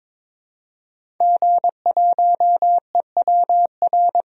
Morse code, heard as a keyed tone, about 23 wpm, 700 Hz. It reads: G1EWR
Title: DXCC-100 Call Signs - 5 Characters - 22wpm